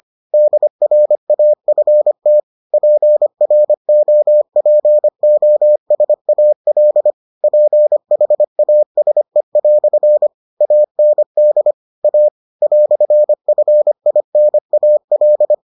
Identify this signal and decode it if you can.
Morse code, heard as a keyed tone, about 25 wpm, 600 Hz. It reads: DRAFT PROPOSAL PHASE" AND A "FINAL